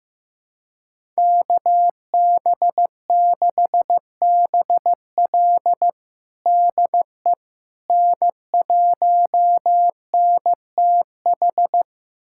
Morse code, heard as a keyed tone, about 16 wpm, 700 Hz. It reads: KB6BL DE N1NTH
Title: USA Callsign DE Callsign - 5 Characters - 15wpm